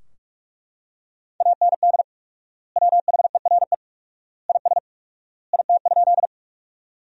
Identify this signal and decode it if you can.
Morse code, heard as a keyed tone, about 41 wpm, 700 Hz. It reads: AND WHERE IS IT?
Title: 4x Word-Spacing - Sentences from Top 200 Words - 45wpm